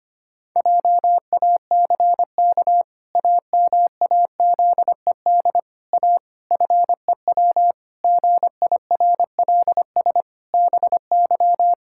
Morse code, heard as a keyed tone, about 25 wpm, 700 Hz. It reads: JACK AMAZED A FEW GIRLS BY
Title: Pangram Challenge v2 - 25wpm